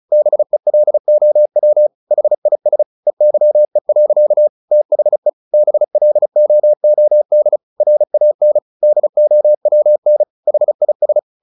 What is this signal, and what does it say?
Morse code, heard as a keyed tone, about 34 wpm, 600 Hz. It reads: BELOW HIS EYE. THE BLOOD RAN DOWN HIS
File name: the old man and the sea / the old man and the sea 1 35wpm